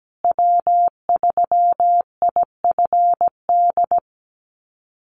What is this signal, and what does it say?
Morse code, heard as a keyed tone, about 17 wpm, 700 Hz. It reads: W3IFD
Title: USA Call Signs Repeated 2x 17wpm